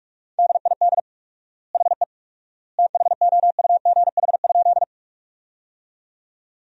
Morse code, heard as a keyed tone, about 45 wpm, 700 Hz. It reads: DID HE THOUGH?
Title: 4x Word-Spacing - Sentences from Top 400 Words - 45wpm